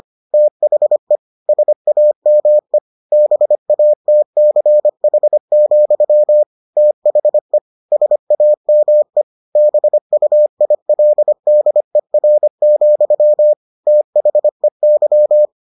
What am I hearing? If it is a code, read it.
Morse code, heard as a keyed tone, about 25 wpm, 600 Hz. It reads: THE SAME BATCH, THE SAME BUILDER, THEY